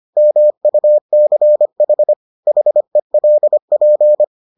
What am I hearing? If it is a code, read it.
Morse code, heard as a keyed tone, about 25 wpm, 600 Hz. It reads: MUCH HELP